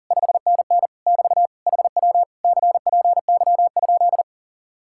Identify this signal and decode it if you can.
Morse code, heard as a keyed tone, about 47 wpm, 700 Hz. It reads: 5NN <BT> HW CPY?
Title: Speed-Racing Common QSO Phrases - 40wpm